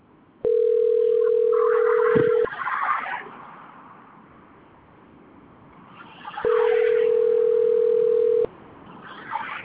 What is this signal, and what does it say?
Ringback tone (North America), 440 + 480 Hz.